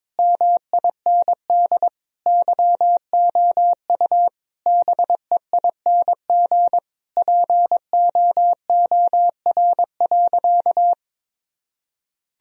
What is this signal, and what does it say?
Morse code, heard as a keyed tone, about 21 wpm, 700 Hz. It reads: MIND YOU BEING POOR.
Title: Sentences from Top 1000 Words - Review Part 3 - 22wpm